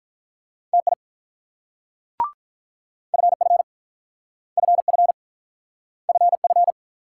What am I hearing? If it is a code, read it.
Morse code, heard as a keyed tone, about 46 wpm, 700 Hz. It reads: TI FF FF FF